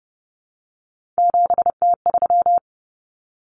Morse code, heard as a keyed tone, about 30 wpm, 700 Hz. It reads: 7T3
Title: Common Cut Numbers - 3-Digit 30wpm